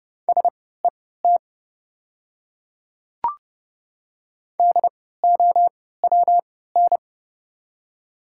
Morse code, heard as a keyed tone, about 30 wpm, spaced as for 17 wpm, 700 Hz. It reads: SET DOWN
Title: 3x Character Spacing - Top 200 Words - 30wpm